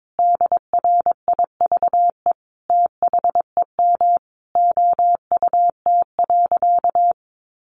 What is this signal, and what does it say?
Morse code, heard as a keyed tone, about 22 wpm, 700 Hz. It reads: DRIVE THEM OUT.
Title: Sentences from Top 500 Words 22wpm